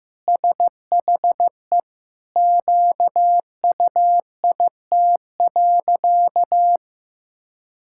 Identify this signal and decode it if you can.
Morse code, heard as a keyed tone, about 15 wpm, 700 Hz. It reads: SHE QUIT.
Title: Pangram Challenge v2 - 15wpm